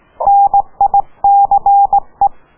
Morse code, heard as a keyed tone, about 18 wpm, 800 Hz. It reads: NICE